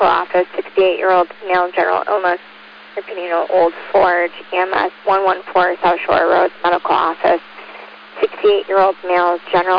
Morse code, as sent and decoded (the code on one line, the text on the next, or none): none